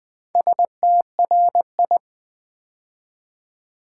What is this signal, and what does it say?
Morse code, heard as a keyed tone, about 20 wpm, 690 Hz. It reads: STRI
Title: Build A Word - 2 Letters At A Time - 6-Letter Words - 20wpm